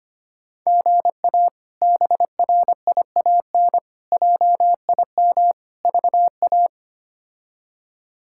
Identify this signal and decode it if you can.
Morse code, heard as a keyed tone, about 25 wpm, 700 Hz. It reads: GA BRIAN JIM VA
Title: K1USN SST QSO Practice - 25wpm